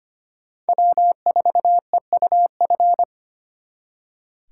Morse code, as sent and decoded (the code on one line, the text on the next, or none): .-- ....- . ..- ..-.
W4EUF